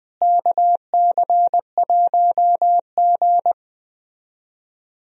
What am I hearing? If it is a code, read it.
Morse code, heard as a keyed tone, about 21 wpm, 700 Hz. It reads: KC1G
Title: USA Call Signs - 4 Characters - Repeated 2x - 20wpm